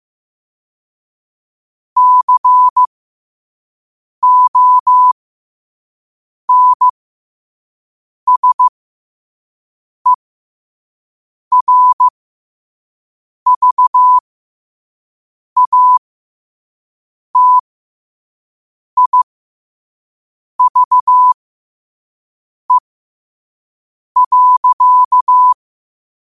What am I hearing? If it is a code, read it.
Morse code, heard as a keyed tone, about 15 wpm, spaced as for 5 wpm, 1000 Hz. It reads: CONSERVATIVE.